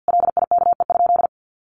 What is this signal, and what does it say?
Morse code, heard as a keyed tone, about 50 wpm, 700 Hz. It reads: LIKE?
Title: Sentences from Top 100 Words 50wpm